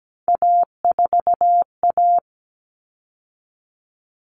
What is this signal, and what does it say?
Morse code, heard as a keyed tone, about 17 wpm, 700 Hz. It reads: A4A
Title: All Cut Numbers - 3-Digit 17wpm